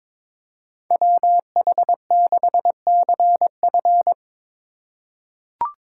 Morse code, heard as a keyed tone, about 22 wpm, 700 Hz. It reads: WH6CF
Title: USA Call Signs - 2x Character Spacing - 22wpm